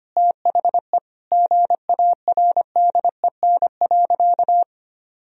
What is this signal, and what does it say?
Morse code, heard as a keyed tone, about 25 wpm, 700 Hz. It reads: THE GARDEN.
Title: Sentences from Top 1000 Words - Review Part 4 - 25wpm